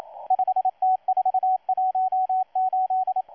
Morse code, heard as a keyed tone, about 28 wpm, 750 Hz, with radio noise. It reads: 5T418
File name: M12-2016-07-07-1310utc-13926khz-error-restart